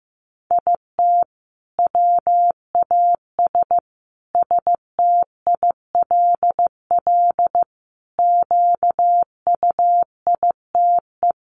Morse code, heard as a keyed tone, about 15 wpm, 700 Hz. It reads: IT WAS STILL QUITE
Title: Sentences from Top 900 Words 15wpm